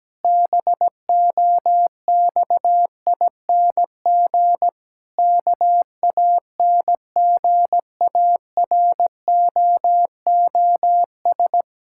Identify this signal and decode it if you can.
Morse code, heard as a keyed tone, about 17 wpm, 700 Hz. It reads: BOXING KANGAROOS
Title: Pangram Challenge v2 - 17wpm